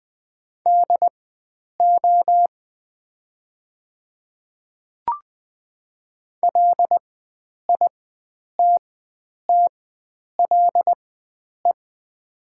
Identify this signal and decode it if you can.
Morse code, heard as a keyed tone, about 20 wpm, spaced as for 9 wpm, 700 Hz. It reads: DO LITTLE
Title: 4x Character Spacing - Top 300 Words - 20wpm